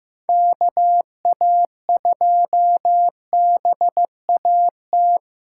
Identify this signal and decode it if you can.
Morse code, heard as a keyed tone, about 15 wpm, 700 Hz. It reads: KA2BAT
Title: USA Call Signs - 6 Characters - 4x Character Spacing - 15wpm